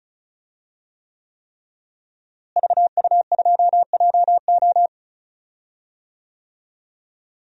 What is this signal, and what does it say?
Morse code, heard as a keyed tone, about 36 wpm, 700 Hz. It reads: VU2JO
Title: DXCC-100 Call Signs - 5 Characters - Repeated 3x - 35wpm